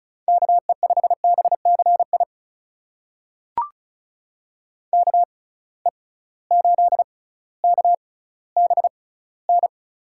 Morse code, heard as a keyed tone, about 36 wpm, spaced as for 18 wpm, 700 Hz. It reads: KE5BCI KE8KBN
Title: USA Call Signs - 6 Characters - 6x Character Spacing - 35wpm